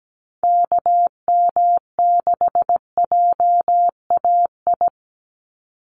Morse code, heard as a keyed tone, about 17 wpm, 700 Hz. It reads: KM6JAI
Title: USA Call Signs 17wpm